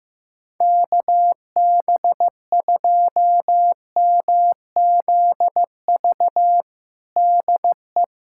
Morse code, heard as a keyed tone, about 15 wpm, 700 Hz. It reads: KB2MZV DE
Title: USA Callsign DE Callsign - 6 Characters - 15wpm